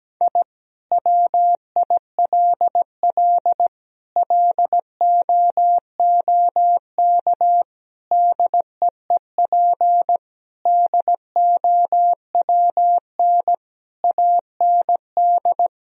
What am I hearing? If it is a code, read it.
Morse code, heard as a keyed tone, about 17 wpm, 700 Hz. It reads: I WILL LOOK DEEP DOWN AND
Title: Sentences from Top 1000 Words - Review Part 3 - 17wpm